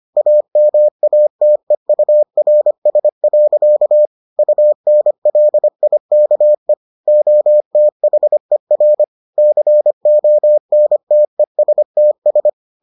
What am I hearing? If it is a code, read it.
Morse code, heard as a keyed tone, about 25 wpm, 600 Hz. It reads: AMATEURS. UNLIKE OTHER CONTESTS